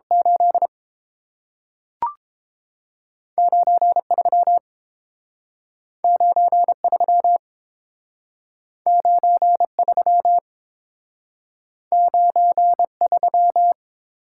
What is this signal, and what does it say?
Morse code, heard as a keyed tone, about 30 wpm, 700 Hz. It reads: 8 9 3 9 3 9 3 9 3